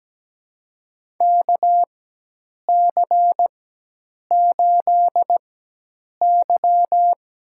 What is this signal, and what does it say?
Morse code, heard as a keyed tone, about 17 wpm, spaced as for 8 wpm, 700 Hz. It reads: KC8Y
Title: USA Call Signs - 4x Character Spacing - 17wpm